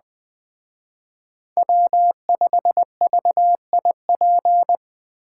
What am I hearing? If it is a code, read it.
Morse code, heard as a keyed tone, about 20 wpm, 700 Hz. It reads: W5VIP
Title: Speed Racing - USA 1x3 Call Signs 17wpm